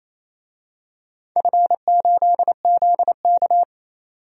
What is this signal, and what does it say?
Morse code, heard as a keyed tone, about 28 wpm, 700 Hz. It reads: F8ZK
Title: DXCC-100 Call Signs - 4 Characters - 28wpm